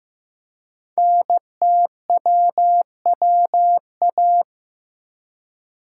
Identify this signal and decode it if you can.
Morse code, heard as a keyed tone, about 15 wpm, 700 Hz. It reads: NTWWA
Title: Code Groups - 5 Letters - 15wpm